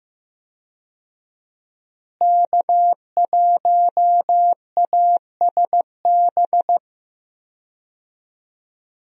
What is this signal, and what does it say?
Morse code, heard as a keyed tone, about 16 wpm, 700 Hz. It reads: K1ASB
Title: USA Call Signs - 5 Characters - Repeated 3x - 15wpm